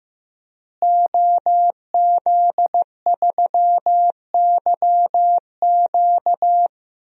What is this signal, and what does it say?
Morse code, heard as a keyed tone, about 15 wpm, 700 Hz. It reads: OZ3YQ
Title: DXCC-100 Call Signs - 5 Characters - Repeated 2x - 15wpm